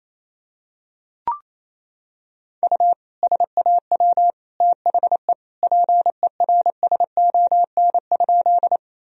Morse code, heard as a keyed tone, about 28 wpm, 700 Hz. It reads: U SAW THE PERSON?
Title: Sentences from Top 400 Words and Abbreviations 28wpm